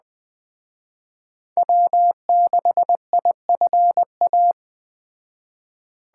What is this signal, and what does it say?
Morse code, heard as a keyed tone, about 20 wpm, 700 Hz. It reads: W6IFA
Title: Speed Racing - USA 1x3 Call Signs 17wpm